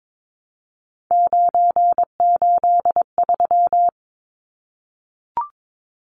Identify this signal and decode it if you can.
Morse code, heard as a keyed tone, about 22 wpm, 700 Hz. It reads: 983
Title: Numbers - 3-Digit 22wpm